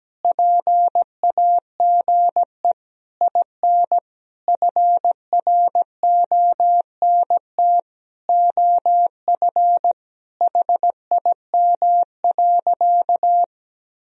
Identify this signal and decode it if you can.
Morse code, heard as a keyed tone, about 17 wpm, 700 Hz. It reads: PAGE IN FRONT OF HIM.
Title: Sentences from Top 1000 Words - Review Part 1 - 17wpm